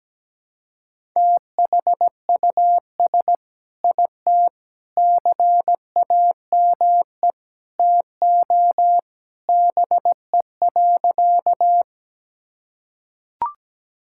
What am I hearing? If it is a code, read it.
Morse code, heard as a keyed tone, about 17 wpm, 700 Hz. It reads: THUS IT CAME TO BE.
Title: Sentences from Top 1000 Words - Review Part 5 - 17wpm